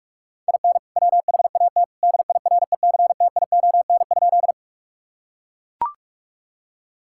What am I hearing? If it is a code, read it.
Morse code, heard as a keyed tone, about 41 wpm, 700 Hz. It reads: IN WHAT DIRECTION?
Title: Sentences from Top 1000 Words - Review Part 5 - 45wpm